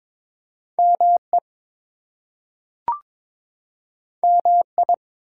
Morse code, heard as a keyed tone, about 22 wpm, 700 Hz. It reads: ME MI
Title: US States and Canadian Provinces 22wpm